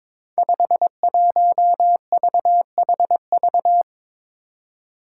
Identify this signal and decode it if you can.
Morse code, heard as a keyed tone, about 22 wpm, 700 Hz. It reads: 51VHV
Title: Five Challenging Letter-Number Combinations 22wpm